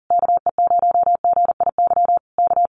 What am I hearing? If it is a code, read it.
Morse code, heard as a keyed tone, about 37 wpm, 700 Hz. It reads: KE0GIY K
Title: Common QSO Phrases Encore 40wpm